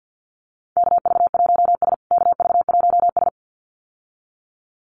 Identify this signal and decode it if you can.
Morse code, heard as a keyed tone, about 50 wpm, 700 Hz. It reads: KV1S KV1S
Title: USA Call Signs Repeated 2x 50wpm